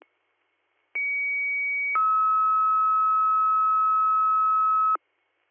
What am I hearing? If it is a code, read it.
Two-tone page, heard as a one-second tone, then a three-second tone, about 2250 then 1290 Hz. It is heard from 1.0 to 5.0 s.